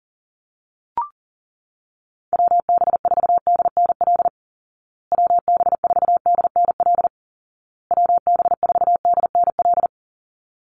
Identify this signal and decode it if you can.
Morse code, heard as a keyed tone, about 40 wpm, 700 Hz. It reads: WB4DNL WB4DNL WB4DNL
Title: Call Signs Repeated 2x - 40wpm